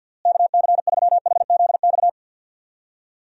Morse code, heard as a keyed tone, about 50 wpm, 700 Hz. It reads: KC3HZX
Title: USA Call Signs - 6 Characters - Repeated 3x - 50wpm